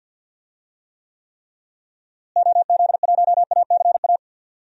Morse code, heard as a keyed tone, about 52 wpm, 700 Hz. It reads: OZ1ACA
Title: DXCC-100 Call Signs - 6 Characters - Repeated 3x - 50wpm